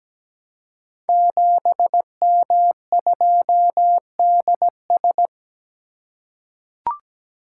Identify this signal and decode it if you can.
Morse code, heard as a keyed tone, about 17 wpm, 700 Hz. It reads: 7M2DS
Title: Code Groups - 5 Letter:Numbers - 17wpm